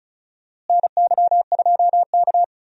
Morse code, heard as a keyed tone, about 34 wpm, 700 Hz. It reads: NY2K
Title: USA Call Signs - 4 Characters - Repeated 3x - 35wpm